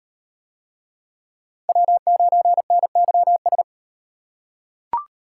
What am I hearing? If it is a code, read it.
Morse code, heard as a keyed tone, about 38 wpm, 700 Hz. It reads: W9NYS